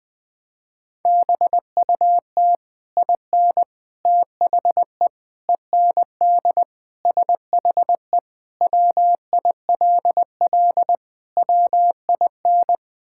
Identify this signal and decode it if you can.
Morse code, heard as a keyed tone, about 21 wpm, 700 Hz. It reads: BUT IN THE END SHE WILL WIN
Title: Sentences with 3-Word Phrases - 20wpm